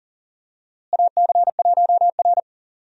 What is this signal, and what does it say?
Morse code, heard as a keyed tone, about 37 wpm, 700 Hz. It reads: AC1R
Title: USA Call Signs - 4 Characters - Repeated 3x - 40wpm